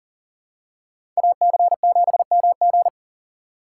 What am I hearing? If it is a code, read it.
Morse code, heard as a keyed tone, about 40 wpm, 700 Hz. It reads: AC7MG
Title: USA Call Signs - 5 Characters - 2x Character Spacing - 40wpm